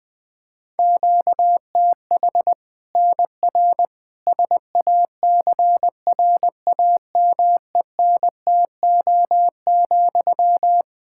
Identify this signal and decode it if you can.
Morse code, heard as a keyed tone, about 20 wpm, 700 Hz. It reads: QTH NR SACRAMENTO,
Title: Practice QSO - Exchange 1 - Beginning - 20wpm